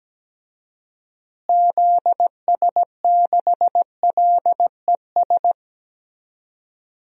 Morse code, heard as a keyed tone, about 17 wpm, 700 Hz. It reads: ZS6LES
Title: DXCC-100 Call Signs - 6 Characters - Repeated 3x - 17wpm